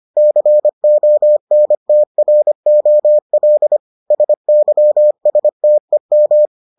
Morse code, heard as a keyed tone, about 25 wpm, 600 Hz. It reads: CONTROL SYSTEM